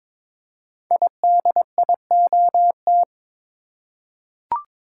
Morse code, heard as a keyed tone, about 22 wpm, 700 Hz. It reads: IDIOT
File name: 5 Letter Words - First 3 Letters Given - 22wpm